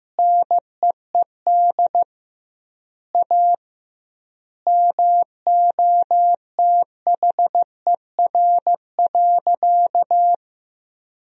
Morse code, heard as a keyed tone, about 15 wpm, 700 Hz. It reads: NEED A MOTHER.
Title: 2x Word-Spacing - Sentences from Top 200 Words - 15wpm